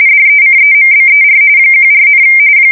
RTTY, about 45.45 baud. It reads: DECEMBER 14,
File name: fsk170 / battleshipnj